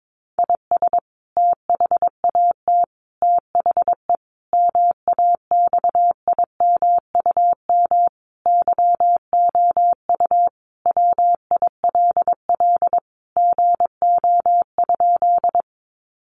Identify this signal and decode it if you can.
Morse code, heard as a keyed tone, about 22 wpm, 700 Hz. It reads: IS THAT THE MAXIMUM YOU WILL GO?
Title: Sentences from Top 1300 Words 22wpm